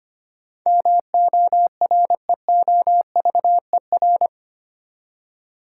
Morse code, heard as a keyed tone, about 25 wpm, 700 Hz. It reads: MOREOVER
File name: Eight Letter Words - 25wpm